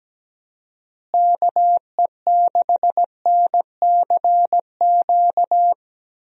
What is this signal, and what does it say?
Morse code, heard as a keyed tone, about 17 wpm, 700 Hz. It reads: KE6NCQ
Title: USA Call Signs - 6 Characters - Repeated 2x - 17wpm